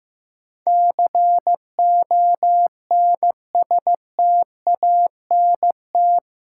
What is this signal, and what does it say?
Morse code, heard as a keyed tone, about 15 wpm, 700 Hz. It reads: CONSTANT
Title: Sentences from Top 1500 Words - New Word Drill - 15wpm